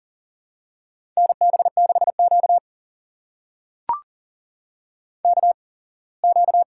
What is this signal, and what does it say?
Morse code, heard as a keyed tone, about 38 wpm, spaced as for 24 wpm, 700 Hz. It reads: NB6Q KQ
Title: USA Call Signs - 4 Characters - 8x Character Spacing - 40wpm